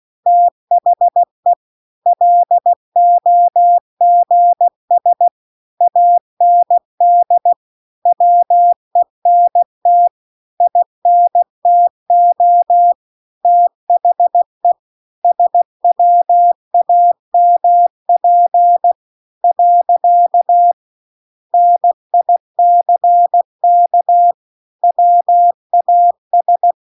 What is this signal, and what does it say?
Morse code, heard as a keyed tone, about 16 wpm, 700 Hz. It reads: THE LOGS AND WENT INTO THE SWAMP . NICK WAS
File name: CW-The-Big-Twoheaded-River-2-16WPM